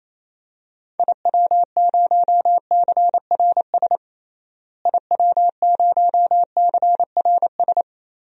Morse code, heard as a keyed tone, about 28 wpm, 700 Hz. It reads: IW0CRS IW0CRS
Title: DXCC-100 Call Signs - 6 Characters - Repeated 3x - 28wpm